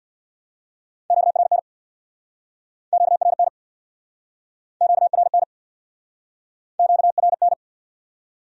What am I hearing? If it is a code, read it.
Morse code, heard as a keyed tone, about 65 wpm, 700 Hz. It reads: QRN QRN QRN QRN